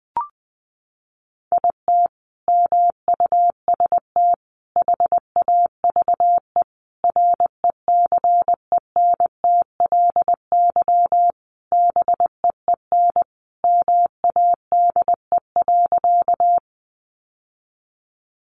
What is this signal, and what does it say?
Morse code, heard as a keyed tone, about 19 wpm, 700 Hz. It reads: IT MUST HAVE RECENTLY BEEN MADE.
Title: Sentences from Top 1200 Words 20wpm